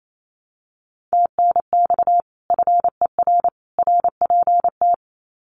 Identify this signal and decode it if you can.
Morse code, heard as a keyed tone, about 28 wpm, 700 Hz. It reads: TNX FER RPT
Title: Common QSO Phrases 28wpm